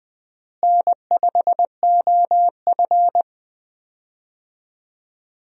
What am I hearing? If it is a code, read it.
Morse code, heard as a keyed tone, about 20 wpm, 700 Hz. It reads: N5OF
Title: USA Call Signs - 4 Characters - Repeated 3x - 20wpm